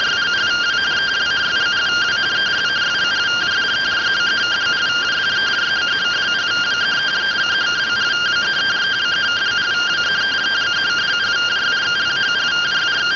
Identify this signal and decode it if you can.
Two-tone teleprinter signal, about 170 Hz shift.